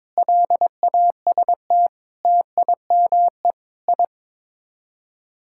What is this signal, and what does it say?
Morse code, heard as a keyed tone, about 22 wpm, 700 Hz. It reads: LAST TIME I
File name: practice / 22wpm / Sets of 7 Words - 22wpm